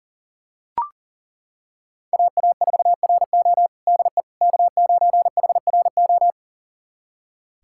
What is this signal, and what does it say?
Morse code, heard as a keyed tone, about 40 wpm, 700 Hz. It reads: AA4RO DE K9HRO
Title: USA Callsign DE Callsign - 5 Characters - 40wpm